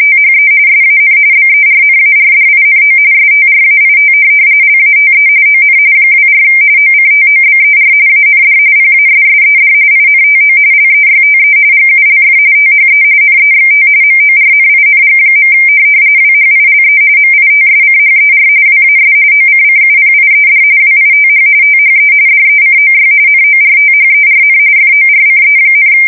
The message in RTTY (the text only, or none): THE SAN BERNARDINO STRAIT LATE ON 24 OCTOBER AND PASSED INTO LEYTE GULF THE NEXT MORNING. WHILE MITSCHER WAS OCCUPIED WITH THE DECOY NORTHERN FORCE,